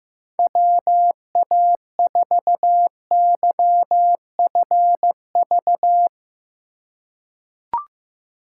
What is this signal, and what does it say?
Morse code, heard as a keyed tone, about 15 wpm, 700 Hz. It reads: WA4YFV
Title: USA Call Signs - 6 Characters - 4x Character Spacing - 15wpm